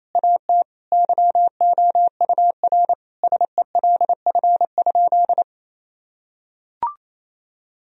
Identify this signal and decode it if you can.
Morse code, heard as a keyed tone, about 28 wpm, 700 Hz. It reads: AT YOUR SELF?
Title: Sentences from Top 1000 Words - Review Part 1 - 28wpm